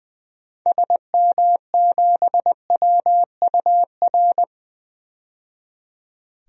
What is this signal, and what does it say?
Morse code, heard as a keyed tone, about 20 wpm, 700 Hz. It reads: SM7WUR
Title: DXCC-100 Call Signs - 6 Characters - 20wpm